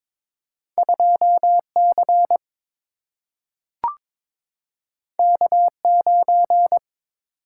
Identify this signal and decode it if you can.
Morse code, heard as a keyed tone, about 22 wpm, 700 Hz. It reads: 2 C K 9
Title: Single-Letter Single-Number 22wpm